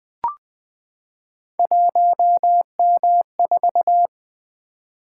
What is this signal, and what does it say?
Morse code, heard as a keyed tone, about 20 wpm, 700 Hz. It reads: 1M4
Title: Three Letter-Number Combinations - 20wpm